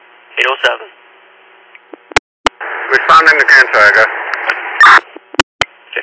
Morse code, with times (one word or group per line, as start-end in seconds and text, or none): none